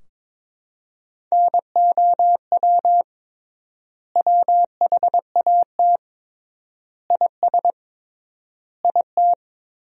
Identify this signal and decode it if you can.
Morse code, heard as a keyed tone, about 22 wpm, 700 Hz. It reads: NOW WHAT IS IT
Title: 3x Word-Spacing - Sentences from Top 200 Words - 22wpm